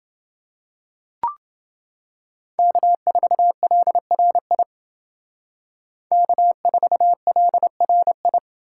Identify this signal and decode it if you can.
Morse code, heard as a keyed tone, about 29 wpm, 700 Hz. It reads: K4LRI K4LRI